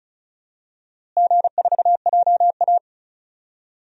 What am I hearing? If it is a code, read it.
Morse code, heard as a keyed tone, about 36 wpm, 700 Hz. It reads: G4JA
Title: DXCC-100 Call Signs - 4 Characters - Repeated 2x - 35wpm